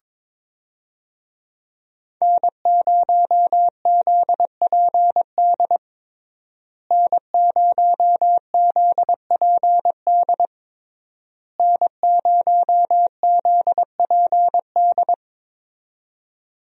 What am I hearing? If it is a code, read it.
Morse code, heard as a keyed tone, about 22 wpm, 700 Hz. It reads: N0ZPD N0ZPD N0ZPD
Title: USA Call Signs - 5 Characters - Repeated 3x - 22wpm